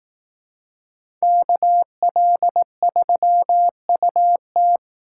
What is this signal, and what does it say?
Morse code, heard as a keyed tone, about 18 wpm, 700 Hz. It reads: KL3UT
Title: Speed Racing - USA Call Signs 15wpm